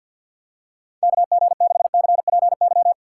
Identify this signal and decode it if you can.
Morse code, heard as a keyed tone, about 52 wpm, 700 Hz. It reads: KG6CPY
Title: USA Call Signs - 6 Characters - 3x Character Spacing - 50wpm